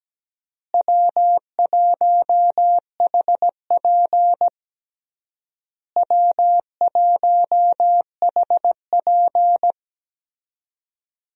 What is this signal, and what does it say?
Morse code, heard as a keyed tone, about 17 wpm, 700 Hz. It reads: W1HP W1HP
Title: USA Call Signs - 4 Characters - Repeated 3x - 17wpm